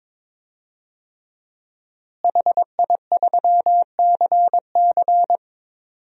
Morse code, heard as a keyed tone, about 22 wpm, 700 Hz. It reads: HI3CC
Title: DXCC-100 Call Signs - 5 Characters - Repeated 3x - 22wpm